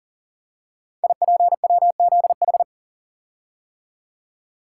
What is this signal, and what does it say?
Morse code, heard as a keyed tone, about 42 wpm, 700 Hz. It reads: IPWZH
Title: Code Groups - 5 Letters - 40wpm